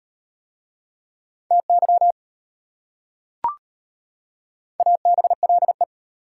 Morse code, heard as a keyed tone, about 38 wpm, 700 Hz. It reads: TY ABLE